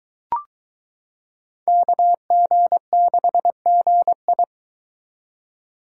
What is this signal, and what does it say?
Morse code, heard as a keyed tone, about 22 wpm, 700 Hz. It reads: KG6GI